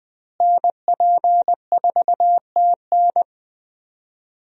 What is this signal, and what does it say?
Morse code, heard as a keyed tone, about 20 wpm, 700 Hz. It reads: NP4TN
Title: USA Call Signs - 5 Characters - Repeated 2x - 20wpm